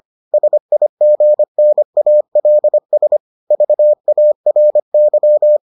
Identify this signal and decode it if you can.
Morse code, heard as a keyed tone, about 25 wpm, 600 Hz. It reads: SIGNALS VARY